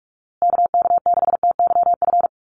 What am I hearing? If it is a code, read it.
Morse code, heard as a keyed tone, about 44 wpm, 700 Hz. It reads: KK6TYF
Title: USA Call Signs Repeated 2x 45wpm